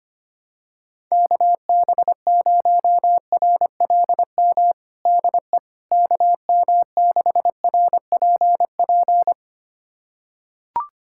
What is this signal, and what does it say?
Morse code, heard as a keyed tone, about 25 wpm, 700 Hz. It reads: KB0RLM DE KM6RPP
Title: USA Callsign DE Callsign - 6 Characters - 25wpm